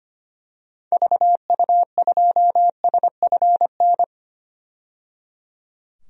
Morse code, heard as a keyed tone, about 25 wpm, 700 Hz. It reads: VU2SFN
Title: DXCC-100 Call Signs - 6 Characters - 25wpm